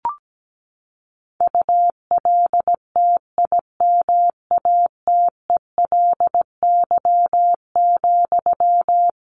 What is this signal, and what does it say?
Morse code, heard as a keyed tone, about 17 wpm, 700 Hz. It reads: ULTIMATELY,
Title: Sentences from Top 1200 Words 17wpm